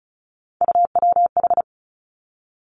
Morse code, heard as a keyed tone, about 36 wpm, 700 Hz. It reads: UWH
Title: Three Challenging Letter-Number Combinations 35wpm